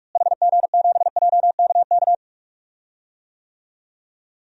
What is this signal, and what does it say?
Morse code, heard as a keyed tone, about 47 wpm, 700 Hz. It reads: HG7JKK
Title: DXCC-100 Call Signs - 6 Characters - Repeated 3x - 45wpm